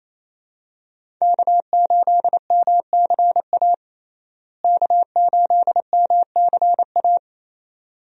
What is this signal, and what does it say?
Morse code, heard as a keyed tone, about 28 wpm, 700 Hz. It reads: K8MCA K8MCA
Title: USA Call Signs - 5 Characters - Repeated 2x - 28wpm